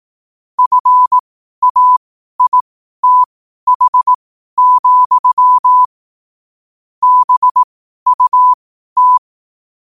Morse code, heard as a keyed tone, about 18 wpm, spaced as for 12 wpm, 1000 Hz. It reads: FAITH, BUT
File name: qotdfiles / 12wpm